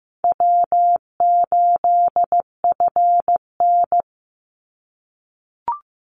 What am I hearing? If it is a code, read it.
Morse code, heard as a keyed tone, about 15 wpm, 700 Hz. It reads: W8FN
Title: Easy Call Signs - 15wpm